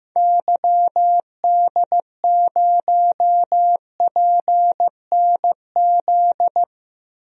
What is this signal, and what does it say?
Morse code, heard as a keyed tone, about 15 wpm, 690 Hz. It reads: YD0PNZ
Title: DXCC-100 Call Signs - 6 Characters - 15wpm